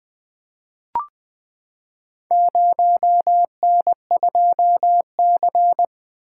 Morse code, heard as a keyed tone, about 20 wpm, 700 Hz. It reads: 0N2C
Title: Four Letter-Number Combinations - 20wpm